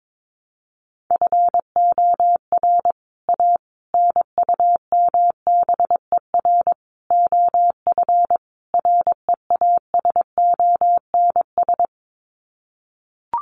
Morse code, heard as a keyed tone, about 22 wpm, 700 Hz. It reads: FOR A NUMBER OF REASONS
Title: Sets of 5 Words Encore2 22wpm